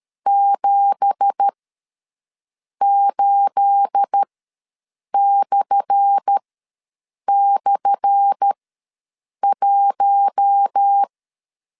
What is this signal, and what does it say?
Morse code, heard as a keyed tone, about 13 wpm, spaced as for 6 wpm, 800 Hz. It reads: 78//1
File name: LEZIONI / 10 SIGNS / 10A SLASH ws 12wpm 3443